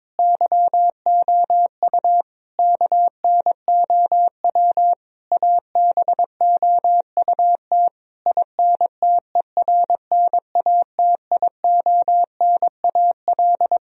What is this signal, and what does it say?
Morse code, heard as a keyed tone, about 22 wpm, 700 Hz. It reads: YOU KNOW ABOUT INTERNATIONAL
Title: Sentences from Top 1000 Words - Review Part 4 - 22wpm